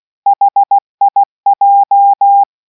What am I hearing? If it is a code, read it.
Morse code, heard as a keyed tone, about 16 wpm, 800 Hz. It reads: HIJ